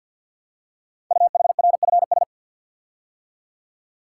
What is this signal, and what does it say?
Morse code, heard as a keyed tone, about 50 wpm, 700 Hz. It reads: UHRFS
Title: Code Groups - 5 Letters - 50wpm